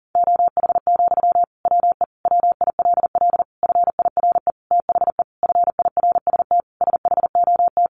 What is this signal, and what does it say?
Morse code, heard as a keyed tone, about 37 wpm, 700 Hz. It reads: OH, WE WILL FIRE THE FIRST SHOT
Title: Sentences from Top 1100 Words 40wpm